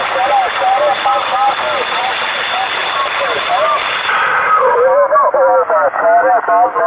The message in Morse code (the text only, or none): none